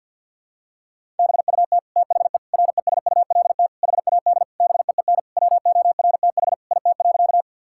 Morse code, heard as a keyed tone, about 51 wpm, 700 Hz. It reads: BUT THE RESULT HAD BEEN WORTH IT.
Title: Sentences from Top 1000 Words - Review Part 5 - 50wpm